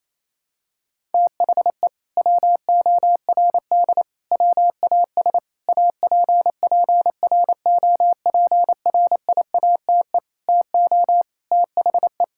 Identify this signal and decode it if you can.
Morse code, heard as a keyed tone, about 28 wpm, 700 Hz. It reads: THE WORD WAS APPROPRIATE TO THE
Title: Sentences from Top 1000 Words - Review Part 5 - 28wpm